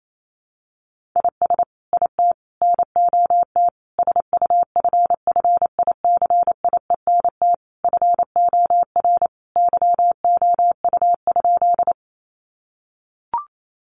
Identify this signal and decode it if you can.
Morse code, heard as a keyed tone, about 28 wpm, 700 Hz. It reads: IS IT NOT SUFFICIENT FOR YOU?
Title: Sentences from Top 1400 Words 28wpm